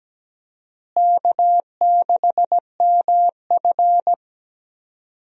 Morse code, heard as a keyed tone, about 17 wpm, 700 Hz. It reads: K6MF
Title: USA Call Signs - 4 Characters - Repeated 3x - 17wpm